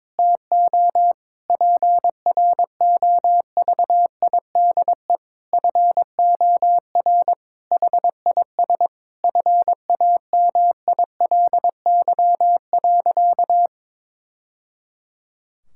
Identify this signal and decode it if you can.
Morse code, heard as a keyed tone, about 22 wpm, 700 Hz. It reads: TO PROVIDE FOR HIS FAMILY.
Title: Sentences from Top 1000 Words - Review Part 5 - 22wpm